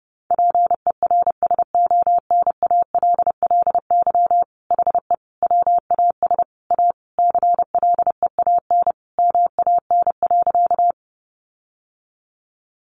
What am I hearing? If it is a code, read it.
Morse code, heard as a keyed tone, about 30 wpm, 700 Hz. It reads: PERSONALLY HE WAS A CLEAN MAN.
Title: Sentences from Top 1400 Words 30wpm